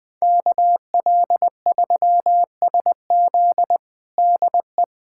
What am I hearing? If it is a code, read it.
Morse code, heard as a keyed tone, about 21 wpm, 700 Hz. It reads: KL3SZ DE
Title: USA Callsign DE Callsign - 5 Characters - 20wpm